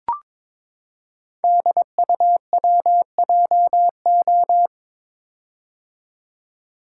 Morse code, heard as a keyed tone, about 22 wpm, 700 Hz. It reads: DUWJO
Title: Code Groups - 5 Letter:Numbers - 22wpm